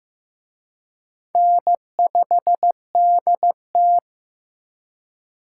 Morse code, heard as a keyed tone, about 18 wpm, 700 Hz. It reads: N5DT
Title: Speed Racing - USA 1x2 Call Signs 15wpm